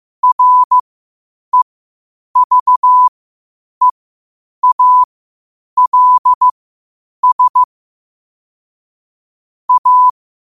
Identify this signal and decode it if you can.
Morse code, heard as a keyed tone, about 15 wpm, spaced as for 8 wpm, 1000 Hz. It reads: REVEALS A
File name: qotdfiles / 8wpm